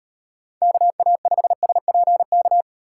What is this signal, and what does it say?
Morse code, heard as a keyed tone, about 38 wpm, 700 Hz. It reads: KA5SPK